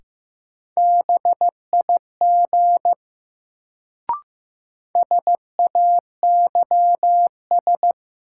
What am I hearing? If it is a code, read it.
Morse code, heard as a keyed tone, about 15 wpm, 700 Hz. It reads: BIG SAYS
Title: Sets of 2 Words - New Word Drill - 15wpm